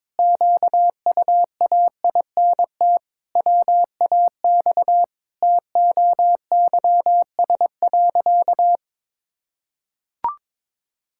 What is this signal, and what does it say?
Morse code, heard as a keyed tone, about 22 wpm, 700 Hz. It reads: QUAINT WAX TOYS.
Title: Pangram Challenge v2 - 22wpm